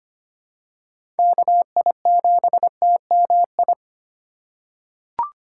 Morse code, heard as a keyed tone, about 25 wpm, 700 Hz. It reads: KI7TMI
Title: USA Call Signs - 6 Characters - 3x Character Spacing - 25wpm